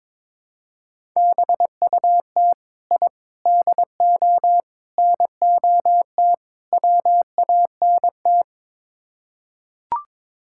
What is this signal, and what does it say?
Morse code, heard as a keyed tone, about 21 wpm, 700 Hz. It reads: BUT I DO NOT WANT
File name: practice / 22wpm / Build A Phrase - 5-Word Phrases - 22wpm